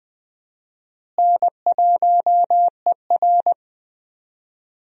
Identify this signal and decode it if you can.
Morse code, heard as a keyed tone, about 21 wpm, 700 Hz. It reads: N1ER
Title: USA Call Signs - 4 Characters - Repeated 3x - 20wpm